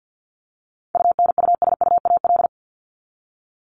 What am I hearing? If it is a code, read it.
Morse code, heard as a keyed tone, about 50 wpm, 700 Hz. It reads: UNUSUAL
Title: Top 1251-1500 Words 50wpm